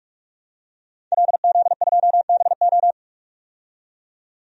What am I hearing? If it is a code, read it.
Morse code, heard as a keyed tone, about 47 wpm, 700 Hz. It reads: LZ2BO
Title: DXCC-100 Call Signs - 5 Characters - 45wpm